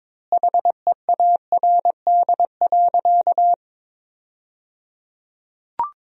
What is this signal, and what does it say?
Morse code, heard as a keyed tone, about 22 wpm, 700 Hz. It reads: HEARD.
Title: 3x Word-Spacing - Sentences from Top 400 Words - 22wpm